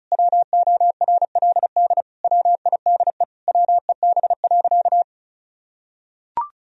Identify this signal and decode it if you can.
Morse code, heard as a keyed tone, about 36 wpm, 700 Hz. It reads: WORLD WIDE WEB.
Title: Sentences from Top 1000 Words - Review Part 4 - 35wpm